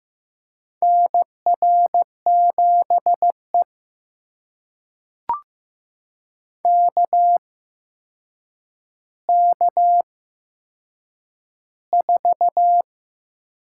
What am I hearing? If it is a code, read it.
Morse code, heard as a keyed tone, about 15 wpm, spaced as for 7 wpm, 700 Hz. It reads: NR7E KK4
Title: USA Call Signs - 8x Character Spacing - 15wpm